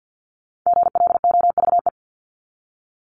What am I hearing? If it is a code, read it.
Morse code, heard as a keyed tone, about 49 wpm, 700 Hz. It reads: GLOVE
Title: Top 1751-2000 Words 50wpm